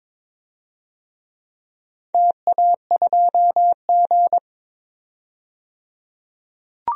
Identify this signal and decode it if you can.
Morse code, heard as a keyed tone, about 21 wpm, 700 Hz. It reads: TA2G
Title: DXCC-100 Call Signs - 4 Characters - 22wpm